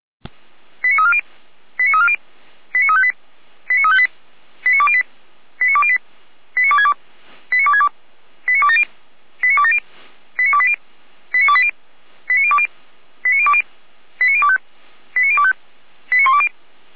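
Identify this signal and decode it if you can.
Five-tone selective call (ZVEI), heard as quick runs of five tones, tones of about 70 ms. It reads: 89340 89340 89358 89358 89208 89208 89262 89262 89270 89270 89280 89280 89020 89020 89025 89025 89120
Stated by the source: Five-tone ZVEI selective call received by SDR, from Feuerwehr Orlamünde.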